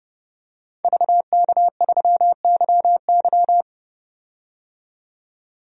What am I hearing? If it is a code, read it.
Morse code, heard as a keyed tone, about 30 wpm, 700 Hz. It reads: VK3YY
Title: DXCC-100 Call Signs - 5 Characters - 30wpm